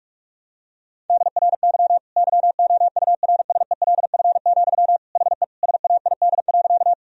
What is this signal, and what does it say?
Morse code, heard as a keyed tone, about 46 wpm, 700 Hz. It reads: DRY YOURSELF, HE SAID.
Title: Sentences from Top 1000 Words - Review Part 3 - 45wpm